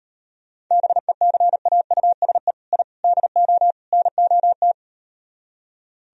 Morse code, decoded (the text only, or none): BECAUSE I DO NOT